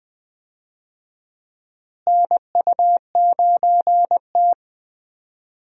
Morse code, heard as a keyed tone, about 20 wpm, 700 Hz. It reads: NU9T
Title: USA Call Signs - 4 Characters - Repeated 2x - 20wpm